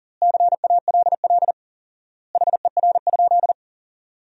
Morse code, heard as a keyed tone, about 37 wpm, 700 Hz. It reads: CALL HER?
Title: 4x Word-Spacing - Sentences from Top 100 Words - 40wpm